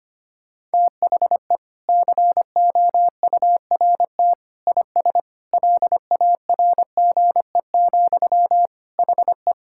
Morse code, heard as a keyed tone, about 25 wpm, 700 Hz. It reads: THE COURT IS LARGE, HE
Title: Sentences from Top 1600 Words 25wpm